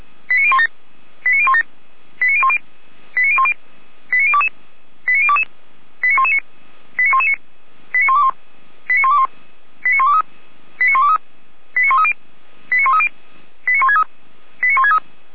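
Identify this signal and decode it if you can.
Five-tone selective call (ZVEI), heard as quick runs of five tones, tones of about 70 ms. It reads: 89017 89017 89010 89010 89022 89022 89119 89119 89121 89121 89123 89123 89130 89130 89163 89163
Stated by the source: ZVEI five-tone selective call received on an SDR.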